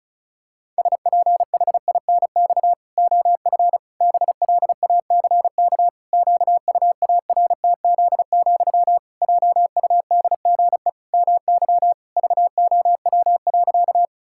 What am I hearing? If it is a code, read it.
Morse code, heard as a keyed tone, about 36 wpm, 700 Hz. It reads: SPHINX OF BLACK QUARTZ, JUDGE MY VOW.
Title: Pangram Challenge v2 - 35wpm